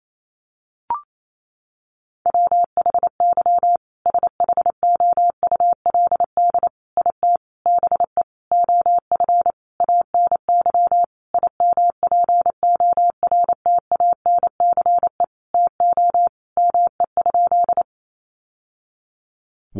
Morse code, heard as a keyed tone, about 28 wpm, 700 Hz. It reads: WHY SHOULD IT BE OF ANY IMPORTANCE TO ME?
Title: Sentences from Top 1200 Words 28wpm